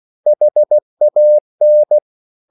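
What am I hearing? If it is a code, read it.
Morse code, heard as a keyed tone, about 16 wpm, 600 Hz. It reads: HAN
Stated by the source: Slow (QRS) Morse code at 16 wpm, from CW med Gnister.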